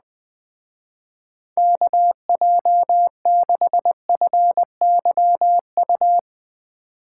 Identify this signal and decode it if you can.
Morse code, heard as a keyed tone, about 20 wpm, 700 Hz. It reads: KJ6FYU
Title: Speed Racing - USA 2x3 Call Signs 17wpm